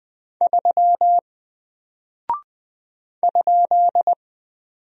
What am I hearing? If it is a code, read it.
Morse code, heard as a keyed tone, about 20 wpm, 700 Hz. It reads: 3 ?
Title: Single Character (Letter:Number:Punctuation:Prosign) Alt 20wpm